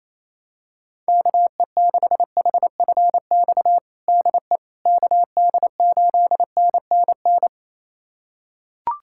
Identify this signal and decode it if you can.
Morse code, heard as a keyed tone, about 28 wpm, 700 Hz. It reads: KE6HFX DE KD8NNN
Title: USA Callsign DE Callsign - 6 Characters - 28wpm